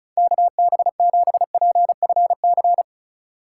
Morse code, heard as a keyed tone, about 36 wpm, 700 Hz. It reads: KB7PFC
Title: USA Call Signs - 6 Characters - 8x Character Spacing - 35wpm